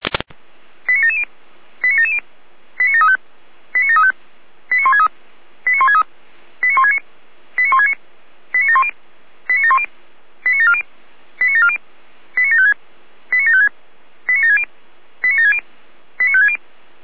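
Five-tone selective call (ZVEI), heard as quick runs of five tones, tones of about 70 ms. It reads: 89770 89770 89735 89735 89163 89163 89169 89169 89710 89710 89740 89740 89756 89756 89760 89760 89560
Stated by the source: Five-tone ZVEI selective call received on an SDR, channel or frequency 455.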